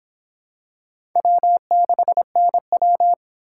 Morse code, heard as a keyed tone, about 25 wpm, 700 Hz. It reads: W6NW